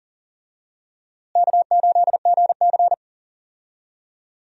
Morse code, heard as a keyed tone, about 40 wpm, 700 Hz. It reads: K8GC
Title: USA Call Signs - 4 Characters - Repeated 3x - 40wpm